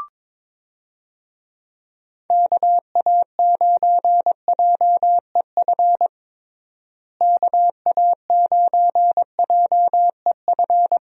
Morse code, heard as a keyed tone, about 22 wpm, 700 Hz. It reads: KA9JEF KA9JEF
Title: USA Call Signs - 6 Characters - Repeated 2x - 22wpm